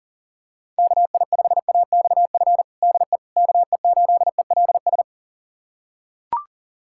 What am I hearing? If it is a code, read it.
Morse code, heard as a keyed tone, about 40 wpm, 700 Hz. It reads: KI5AXF DE KE8ELS
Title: USA Callsign DE Callsign - 6 Characters - 40wpm